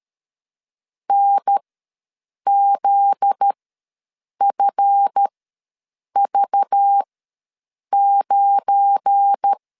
Morse code, heard as a keyed tone, about 13 wpm, spaced as for 7 wpm, 800 Hz. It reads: NZFV9
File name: LEZIONI / 10 SIGNS / 10A SLASH ws 12wpm 3443